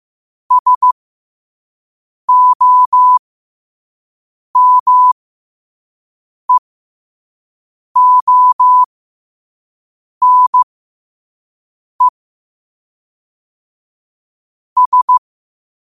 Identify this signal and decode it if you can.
Morse code, heard as a keyed tone, about 15 wpm, spaced as for 5 wpm, 1000 Hz. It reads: SOMEONE S
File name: qotdfiles / 5wpm